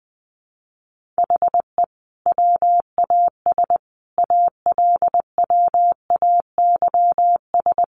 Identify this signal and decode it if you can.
Morse code, heard as a keyed tone, about 20 wpm, 700 Hz. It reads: HE WAS ALWAYS
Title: Sentences from Top 1300 Words 20wpm